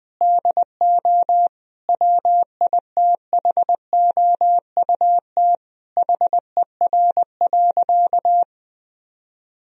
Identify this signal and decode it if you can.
Morse code, heard as a keyed tone, about 20 wpm, 700 Hz. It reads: DO WITHOUT HER.
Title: Sentences from Top 1000 Words - Review Part 3 - 20wpm